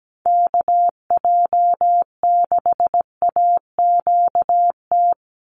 Morse code, heard as a keyed tone, about 17 wpm, 700 Hz. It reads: KJ6AQT
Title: USA Call Signs 17wpm